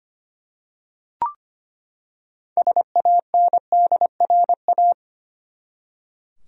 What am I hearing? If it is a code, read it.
Morse code, heard as a keyed tone, about 25 wpm, 700 Hz. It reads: SANDRA
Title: USA Common Names - 25wpm